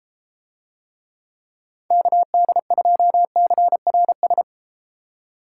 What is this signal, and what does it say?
Morse code, heard as a keyed tone, about 31 wpm, 700 Hz. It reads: KD2CRS